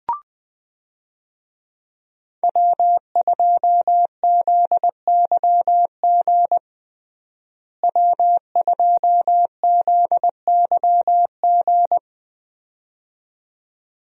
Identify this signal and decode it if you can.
Morse code, heard as a keyed tone, about 20 wpm, 700 Hz. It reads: W2ZYG W2ZYG
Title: USA Call Signs - 5 Characters - Repeated 2x - 20wpm